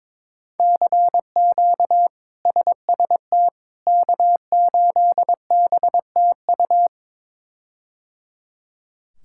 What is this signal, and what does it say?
Morse code, heard as a keyed tone, about 22 wpm, 700 Hz. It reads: CQ SST K8BTU
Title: K1USN SST QSO Practice - 22wpm